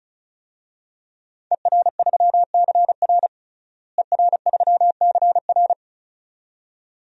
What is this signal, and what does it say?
Morse code, heard as a keyed tone, about 36 wpm, 700 Hz. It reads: ER3CR ER3CR
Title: DXCC-100 Call Signs - 5 Characters - Repeated 2x - 35wpm